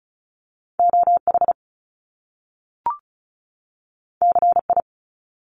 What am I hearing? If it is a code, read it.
Morse code, heard as a keyed tone, about 36 wpm, 700 Hz. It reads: OH CI
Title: Top Two Letter Combinations in Words 35wpm